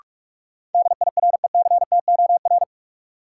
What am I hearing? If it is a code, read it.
Morse code, heard as a keyed tone, about 46 wpm, 700 Hz. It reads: DIRECTOR
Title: Sets of 3 Words - New Word Drill - 45wpm